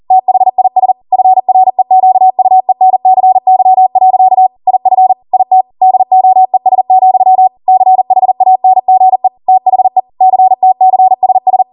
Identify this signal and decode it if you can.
Morse code, heard as a keyed tone, about 40 wpm, 750 Hz. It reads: THIS FREQUENCY. IF IT DOES, CHANGE THE CTCSS
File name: Morse / 40WPM